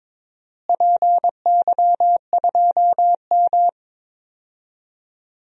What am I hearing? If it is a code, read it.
Morse code, heard as a keyed tone, about 21 wpm, 700 Hz. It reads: PY2M
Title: DXCC-100 Call Signs - 4 Characters - Repeated 2x - 22wpm